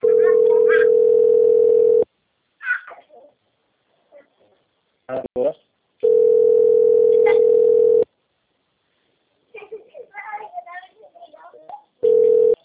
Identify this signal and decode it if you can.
Ringback tone (North America), 440 + 480 Hz, with speech.
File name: out- -2016- - - .65060